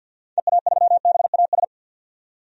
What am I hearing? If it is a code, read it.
Morse code, heard as a keyed tone, about 46 wpm, 700 Hz. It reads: EA3BAS
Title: DXCC-100 Call Signs - 6 Characters - 50wpm